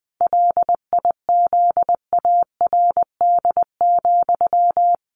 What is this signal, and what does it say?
Morse code, heard as a keyed tone, about 20 wpm, 700 Hz. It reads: LIZARD,
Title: A Sound Of Thunder 20wpm